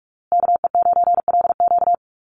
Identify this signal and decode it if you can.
Morse code, heard as a keyed tone, about 44 wpm, 700 Hz. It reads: KE9LQ
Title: USA Call Signs Repeated 3x 45wpm